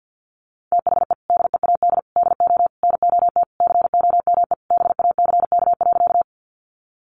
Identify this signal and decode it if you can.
Morse code, heard as a keyed tone, about 50 wpm, 700 Hz. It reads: THE DEAD DO NOT COME BACK.
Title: Sentences from Top 1500 Words 50wpm